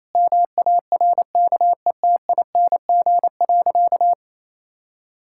Morse code, heard as a keyed tone, about 28 wpm, 700 Hz. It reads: MARKETING.
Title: Sentences from Top 1000 Words - Review Part 5 - 28wpm